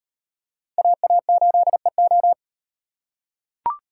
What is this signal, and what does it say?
Morse code, heard as a keyed tone, about 34 wpm, 700 Hz. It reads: AA8EO